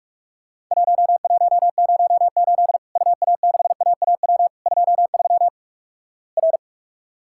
Morse code, heard as a keyed tone, about 46 wpm, 700 Hz. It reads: 1108 UA6AAW 23 R
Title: Worked All Europe DX Contest - QTC Exchange - 45wpm